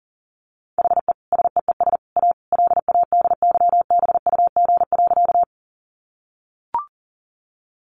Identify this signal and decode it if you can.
Morse code, heard as a keyed tone, about 40 wpm, 700 Hz. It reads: HE SEES A LADYBUG.
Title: Additional Easy Phrases 40wpm